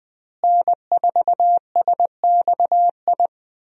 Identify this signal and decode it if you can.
Morse code, heard as a keyed tone, about 20 wpm, 700 Hz. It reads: N4SXI
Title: USA Call Signs - 5 Characters - Repeated 3x - 20wpm